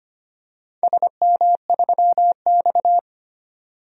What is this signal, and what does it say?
Morse code, heard as a keyed tone, about 25 wpm, 700 Hz. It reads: SM3X
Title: DXCC-100 Call Signs - 4 Characters - 25wpm